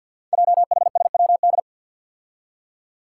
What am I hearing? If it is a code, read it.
Morse code, heard as a keyed tone, about 48 wpm, 700 Hz. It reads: JHSWD
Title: Code Groups - 5 Letters - 50wpm